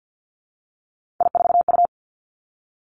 Morse code, heard as a keyed tone, about 52 wpm, 700 Hz. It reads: I4U
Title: Three Challenging Letter-Number Combinations 50wpm